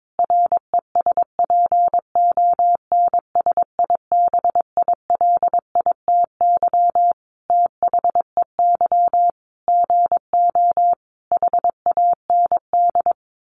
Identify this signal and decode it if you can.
Morse code, heard as a keyed tone, about 22 wpm, 700 Hz. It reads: RESPONSIBILITY THEY GO HAND
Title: Sentences from Top 1200 Words 22wpm